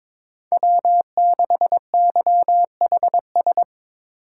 Morse code, heard as a keyed tone, about 22 wpm, 700 Hz. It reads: W6YHS
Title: USA Call Signs - 5 Characters - 2x Character Spacing - 22wpm